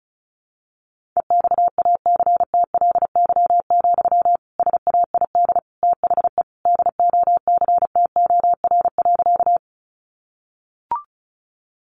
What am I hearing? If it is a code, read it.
Morse code, heard as a keyed tone, about 34 wpm, 700 Hz. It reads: EXACTLY, SAID THE DOCTOR.
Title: Sentences from Top 1500 Words 35wpm